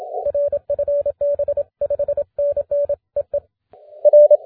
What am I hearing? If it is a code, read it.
Morse code, heard as a keyed tone, about 28 wpm, 590 Hz. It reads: RFB 5NN EE R
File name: wp / jw8hga 20m cw 20 may 2011 1850z